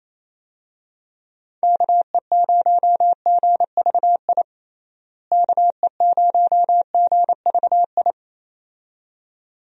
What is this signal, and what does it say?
Morse code, heard as a keyed tone, about 28 wpm, 700 Hz. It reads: KE0GVI KE0GVI
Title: USA Call Signs - 6 Characters - Repeated 2x - 28wpm